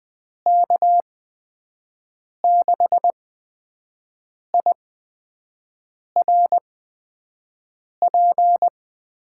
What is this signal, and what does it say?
Morse code, heard as a keyed tone, about 20 wpm, spaced as for 5 wpm, 700 Hz. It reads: K6IRP
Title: USA Call Signs - 6 Characters - 8x Character Spacing - 20wpm